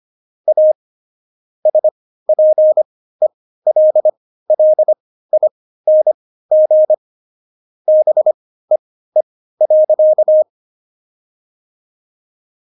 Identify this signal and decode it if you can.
Morse code, heard as a keyed tone, about 25 wpm, spaced as for 15 wpm, 620 Hz. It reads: A SPELLING BEE.
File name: Jokes-25-char-15-eff